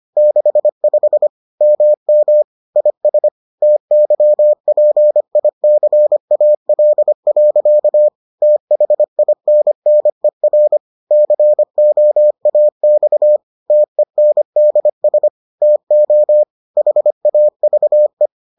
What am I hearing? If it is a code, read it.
Morse code, heard as a keyed tone, about 25 wpm, 600 Hz. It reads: 65 MM IS TYPICAL. THINNER COAX TENDS TO HAVE